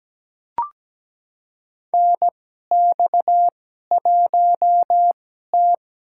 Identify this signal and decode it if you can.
Morse code, heard as a keyed tone, about 17 wpm, spaced as for 12 wpm, 700 Hz. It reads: NX1T
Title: USA Call Signs - 2x Character Spacing - 17wpm